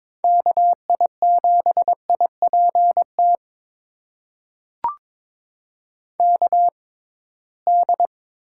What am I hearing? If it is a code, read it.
Morse code, heard as a keyed tone, about 22 wpm, spaced as for 17 wpm, 700 Hz. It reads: KI7IPT KD
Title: USA Call Signs - 6 Characters - 6x Character Spacing - 22wpm